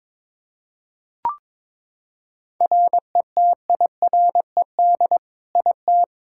Morse code, heard as a keyed tone, about 22 wpm, 700 Hz. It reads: RETIRED IT
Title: Practice QSO - Exchange 3 - More Info - 22wpm